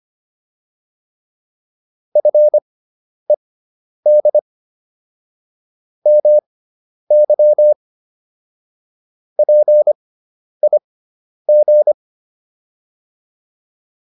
Morse code, heard as a keyed tone, about 25 wpm, spaced as for 10 wpm, 600 Hz. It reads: FED MY PIG